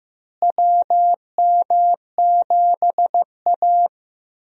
Morse code, heard as a keyed tone, about 15 wpm, 700 Hz. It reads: WM7A
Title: USA Call Signs - 4 Characters - Repeated 3x - 15wpm